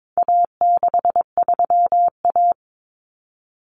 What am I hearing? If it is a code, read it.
Morse code, heard as a keyed tone, about 22 wpm, 700 Hz. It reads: A63A
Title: Common Cut Numbers - 4-Digit 22wpm